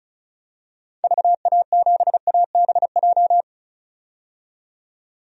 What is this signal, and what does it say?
Morse code, heard as a keyed tone, about 36 wpm, 700 Hz. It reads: VA7ABJ
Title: DXCC-100 Call Signs - 6 Characters - 35wpm